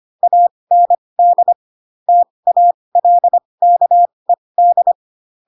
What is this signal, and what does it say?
Morse code, heard as a keyed tone, about 25 wpm, spaced as for 20 wpm, 700 Hz. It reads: AND TALKED